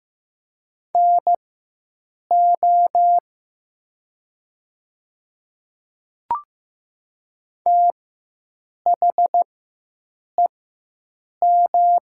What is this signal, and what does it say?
Morse code, heard as a keyed tone, about 15 wpm, spaced as for 7 wpm, 700 Hz. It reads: NO THEM
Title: 4x Character Spacing - Top 100 Words - 15wpm